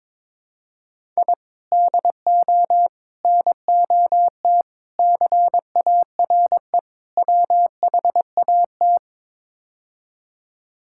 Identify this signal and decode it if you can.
Morse code, heard as a keyed tone, about 22 wpm, 700 Hz. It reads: I DO NOT CARE WHAT
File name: practice / 22wpm / Build A Phrase - 5-Word Phrases - 22wpm